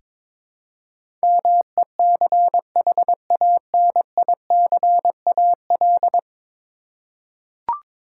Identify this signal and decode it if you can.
Morse code, heard as a keyed tone, about 22 wpm, 700 Hz. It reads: MECHANICAL
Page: morse-code-ninja-practice-22wpm